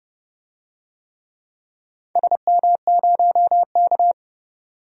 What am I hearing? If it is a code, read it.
Morse code, heard as a keyed tone, about 30 wpm, 700 Hz. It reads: SM0K
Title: DXCC-100 Call Signs - 4 Characters - 30wpm